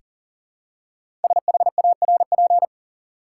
Morse code, heard as a keyed tone, about 40 wpm, 700 Hz. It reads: SHARP
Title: Five Letter Words - 40wpm